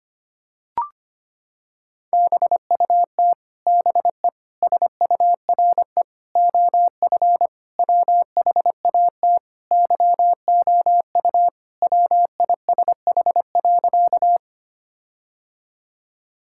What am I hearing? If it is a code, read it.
Morse code, heard as a keyed tone, about 25 wpm, 700 Hz. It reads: BUT BE SURE OF WHAT YOU WISH.
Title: Sentences from Top 1600 Words 25wpm